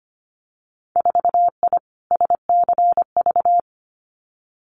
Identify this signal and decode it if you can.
Morse code, heard as a keyed tone, about 25 wpm, 700 Hz. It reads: 4I SCV
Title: Winter Field Day Exchanges Encore 25wpm